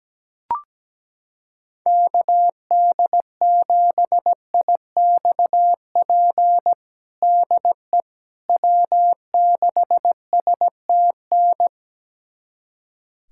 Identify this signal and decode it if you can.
Morse code, heard as a keyed tone, about 17 wpm, 700 Hz. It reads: KD7IXP DE W6STN
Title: USA Callsign DE Callsign 17wpm